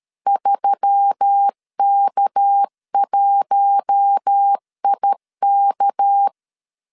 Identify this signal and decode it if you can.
Morse code, heard as a keyed tone, about 13 wpm, 800 Hz. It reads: 3K1IK
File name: LEZIONI / 6 PKX0 / 6 PK 12wpm 2006